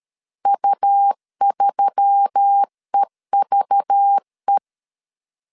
Morse code, heard as a keyed tone, about 13 wpm, 800 Hz. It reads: U3EVE
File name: CW / DET / 1 9 1500C S B 12wpm 3819